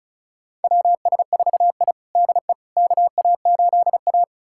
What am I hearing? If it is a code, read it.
Morse code, heard as a keyed tone, about 34 wpm, 700 Hz. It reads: WS4I DE KA8A
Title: USA Callsign DE Callsign - 4 Characters - 35wpm